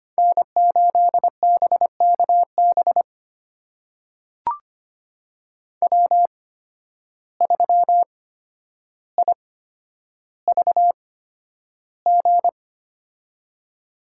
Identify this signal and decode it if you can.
Morse code, heard as a keyed tone, about 25 wpm, spaced as for 11 wpm, 700 Hz. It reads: N8BKB W3IVG
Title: USA Call Signs - 5 Characters - 8x Character Spacing - 25wpm